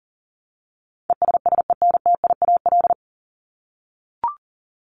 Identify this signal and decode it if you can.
Morse code, heard as a keyed tone, about 40 wpm, 700 Hz. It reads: ESSENTIAL
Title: LY Suffix - Top 100 Words with Base Words 40wpm